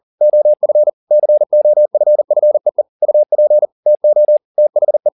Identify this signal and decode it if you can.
Morse code, heard as a keyed tone, about 40 wpm, 600 Hz. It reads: OF COFFEE UP TO THE
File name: the old man and the sea 2 40wpm